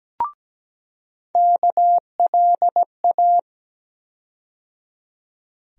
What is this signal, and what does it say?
Morse code, heard as a keyed tone, about 17 wpm, 700 Hz. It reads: KLA
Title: USA Call Sign Postfixes - 3 Letters 17wpm